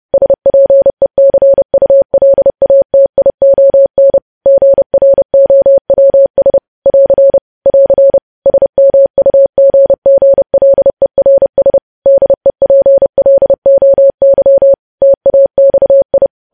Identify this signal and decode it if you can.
Morse code, heard as a keyed tone, about 30 wpm, 550 Hz. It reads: SPECULATION GROWS <AR> <AR> SMUGGLERS DEPLOY TAXI
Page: news headlines in morse code at 30 wpm